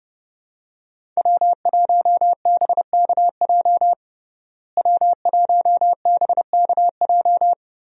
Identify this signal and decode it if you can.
Morse code, heard as a keyed tone, about 30 wpm, 700 Hz. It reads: W1BKJ W1BKJ
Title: USA Call Signs - 5 Characters - Repeated 2x - 30wpm